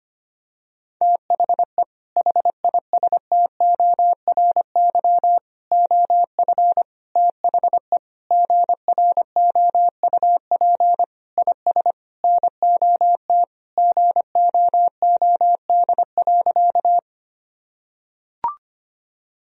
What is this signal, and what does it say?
Morse code, heard as a keyed tone, about 25 wpm, 700 Hz. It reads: THE HISTORY OF THE GROUP IS NOT GOOD.
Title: Sentences from Top 1000 Words - Review Part 1 - 25wpm